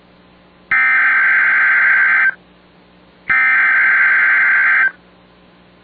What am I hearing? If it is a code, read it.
Emergency Alert System (SAME) tones (header bursts), heard as harsh buzzing bursts. It is heard from 0.7 to 4.8 s.